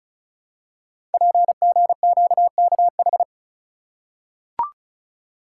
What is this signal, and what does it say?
Morse code, heard as a keyed tone, about 36 wpm, 700 Hz. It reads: PGQKH
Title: Code Groups - 5 Letters - 35wpm